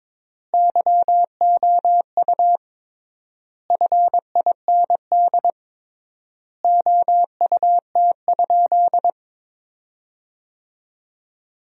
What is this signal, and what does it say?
Morse code, heard as a keyed tone, about 22 wpm, 700 Hz. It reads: YOU FIND OUT?
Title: 3x Word-Spacing - Sentences from Top 200 Words - 22wpm